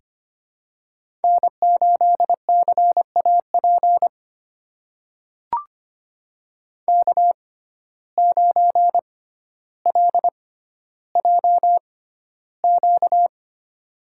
Morse code, heard as a keyed tone, about 25 wpm, spaced as for 13 wpm, 700 Hz. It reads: N8CAP K9LJQ
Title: USA Call Signs - 5 Characters - 6x Character Spacing - 25wpm